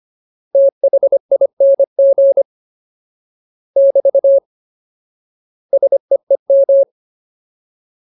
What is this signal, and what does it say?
Morse code, heard as a keyed tone, about 25 wpm, 550 Hz. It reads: THING <BT> SEEM
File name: Ultimate Word List / 25WPM Ultimate